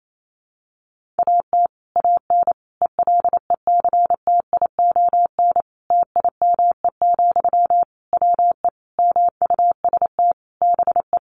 Morse code, heard as a keyed tone, about 28 wpm, 700 Hz. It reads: AT AN ELECTION TIME, WE MUST BE
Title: Sentences from Top 1500 Words 28wpm